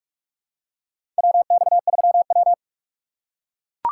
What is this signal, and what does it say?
Morse code, heard as a keyed tone, about 43 wpm, 700 Hz. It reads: WX3W
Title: USA Call Signs - 4 Characters - 4x Character Spacing - 45wpm